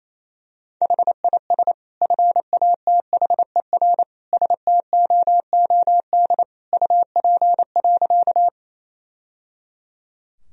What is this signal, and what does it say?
Morse code, heard as a keyed tone, about 28 wpm, 700 Hz. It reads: HIS FATHER STOOD UP.
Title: Sentences from Top 1000 Words - Review Part 2 - 28wpm